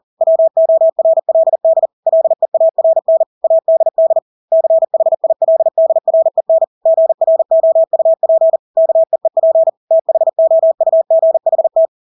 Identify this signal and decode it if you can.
Morse code, heard as a keyed tone, about 40 wpm, 650 Hz. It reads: WORLD LEARN ADD CHILDREN GROUP KEEP THOUGHT
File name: Engelske ord 40-40 wpm